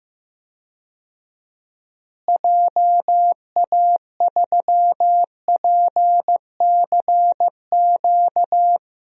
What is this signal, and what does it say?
Morse code, heard as a keyed tone, about 15 wpm, 700 Hz. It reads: JA3PCQ
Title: DXCC-100 Call Signs - 6 Characters - Repeated 3x - 15wpm